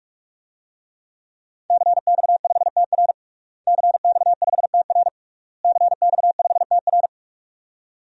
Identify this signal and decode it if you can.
Morse code, heard as a keyed tone, about 46 wpm, 700 Hz. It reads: CX5TR CX5TR CX5TR
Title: DXCC-100 Call Signs - 5 Characters - Repeated 3x - 45wpm